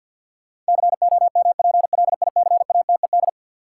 Morse code, heard as a keyed tone, about 50 wpm, 700 Hz. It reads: COMPLICATED
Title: Sentences from Top 1200 Words - New Word Drill - 50wpm